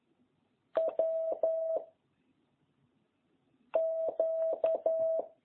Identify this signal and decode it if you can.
Morse code, heard as a keyed tone, about 11 wpm, 650 Hz. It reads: W Q